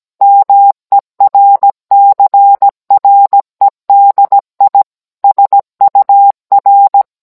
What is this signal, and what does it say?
Morse code, heard as a keyed tone, about 17 wpm, 800 Hz. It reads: MERCREDI SUR